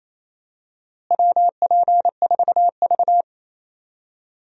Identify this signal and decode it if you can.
Morse code, heard as a keyed tone, about 28 wpm, 700 Hz. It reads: WP4V
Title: USA Call Signs - 4 Characters - Repeated 3x - 28wpm